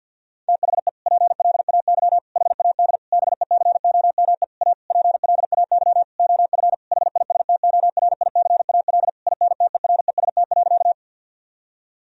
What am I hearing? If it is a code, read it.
Morse code, heard as a keyed tone, about 50 wpm, 700 Hz. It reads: THE PLAY HAD BECOME A PLAY OF HISTORICAL INTEREST.
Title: Sentences from Top 1000 Words - Review Part 4 - 50wpm